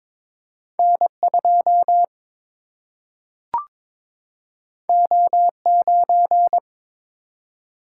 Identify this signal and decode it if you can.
Morse code, heard as a keyed tone, about 22 wpm, 700 Hz. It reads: N 2 O 9
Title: Single-Letter Single-Number 22wpm